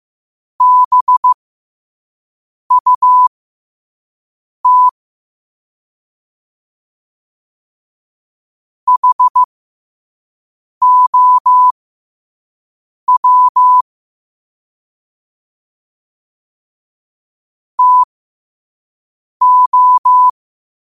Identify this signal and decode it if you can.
Morse code, heard as a keyed tone, about 15 wpm, spaced as for 5 wpm, 1000 Hz. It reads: BUT HOW TO
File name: qotdfiles / 5wpm